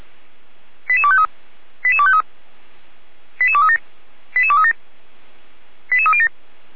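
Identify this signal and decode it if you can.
Five-tone selective call (ZVEI), heard as quick runs of five tones, tones of about 70 ms. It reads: 88253 88253 88237 88237 88307
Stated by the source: ZVEI five-tone selective call 88307, received by SDR, from feuerwehr-orlamuende.org (Feuerwehr Orlamünde).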